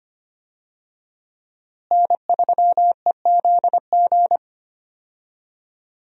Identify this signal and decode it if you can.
Morse code, heard as a keyed tone, about 25 wpm, 700 Hz. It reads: N3EZG
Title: USA Call Signs - 5 Characters - Repeated 3x - 25wpm